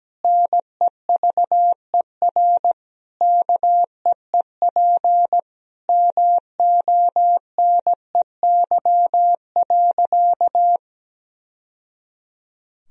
Morse code, heard as a keyed tone, about 17 wpm, 700 Hz. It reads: NEVER KEEP MONEY.
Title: Sentences from Top 1000 Words - Review Part 4 - 17wpm